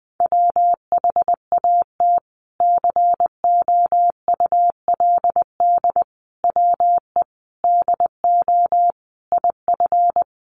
Morse code, heard as a keyed tone, about 20 wpm, 700 Hz. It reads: WHAT COULD WE DO IF
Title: Sentences from Top 400 Words 20wpm